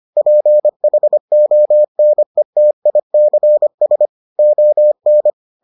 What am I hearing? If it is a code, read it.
Morse code, heard as a keyed tone, about 25 wpm, 600 Hz. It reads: PHONETICS ON